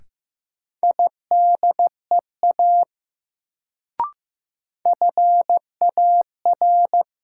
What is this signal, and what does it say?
Morse code, heard as a keyed tone, about 15 wpm, 700 Hz. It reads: IDEA FAR
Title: Sets of 3 Words - New Word Drill - 15wpm